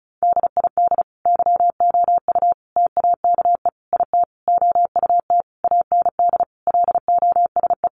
Morse code, heard as a keyed tone, about 34 wpm, 700 Hz. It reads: DID YOU TAKE IT OUT AND LOSE
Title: Sentences from Top 1100 Words 35wpm